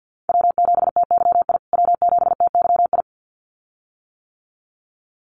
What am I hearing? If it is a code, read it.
Morse code, heard as a keyed tone, about 50 wpm, 700 Hz. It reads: W7TYI W7TYI
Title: USA Call Signs Repeated 2x 50wpm